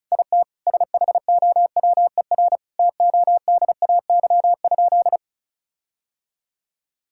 Morse code, heard as a keyed tone, about 34 wpm, 700 Hz. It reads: IT SHOWER TODAY?
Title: Sentences from Top 1600 Words 35wpm